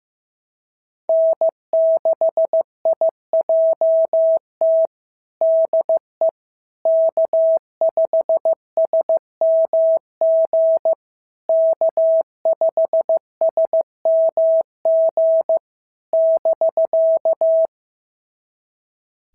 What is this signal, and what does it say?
Morse code, heard as a keyed tone, about 15 wpm, 650 Hz. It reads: N6IJT DE K5SMG K5SMG <BK>
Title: SOTA QSO Practice 15wpm